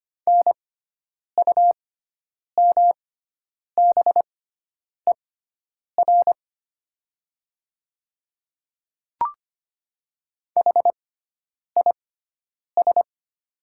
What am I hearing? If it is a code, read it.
Morse code, heard as a keyed tone, about 25 wpm, spaced as for 9 wpm, 700 Hz. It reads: NUMBER HIS
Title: 6x Character Spacing - Top 100 Words - 25wpm